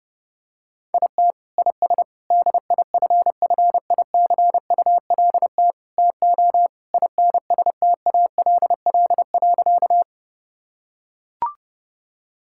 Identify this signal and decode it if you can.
Morse code, heard as a keyed tone, about 30 wpm, 700 Hz. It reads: IT IS DIFFICULT TO INSTALL.
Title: Sentences from Top 1600 Words 30wpm